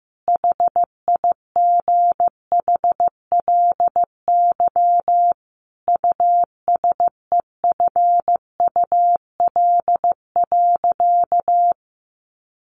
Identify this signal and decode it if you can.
Morse code, heard as a keyed tone, about 14 wpm, 700 Hz. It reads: HIGHLY USEFUL.
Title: Sentences from Top 1100 Words 15wpm